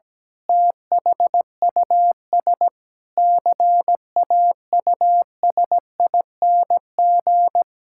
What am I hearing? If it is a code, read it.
Morse code, heard as a keyed tone, about 17 wpm, 700 Hz. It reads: THUS CAUSING
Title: Pangram Challenge v2 - 17wpm